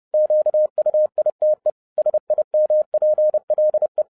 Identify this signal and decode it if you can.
Morse code, heard as a keyed tone, about 30 wpm, 610 Hz. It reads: QUITE SIMPLE
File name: 2011-07-mp3 / n20 GREETINGS FROM SINGAPORE 30wpm